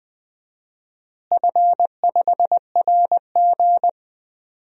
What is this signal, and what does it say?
Morse code, heard as a keyed tone, about 20 wpm, 700 Hz. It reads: F5RG
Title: DXCC-100 Call Signs - 4 Characters - Repeated 3x - 20wpm